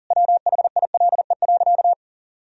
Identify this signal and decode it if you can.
Morse code, heard as a keyed tone, about 40 wpm, 700 Hz. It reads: WHILE.
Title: Sentences from Top 1000 Words - Review Part 1 - 40wpm